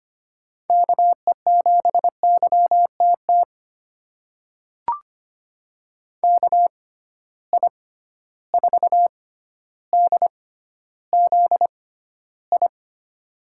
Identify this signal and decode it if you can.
Morse code, heard as a keyed tone, about 25 wpm, spaced as for 13 wpm, 700 Hz. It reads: KE7YTT KI4DZI
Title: USA Call Signs - 6 Characters - 6x Character Spacing - 25wpm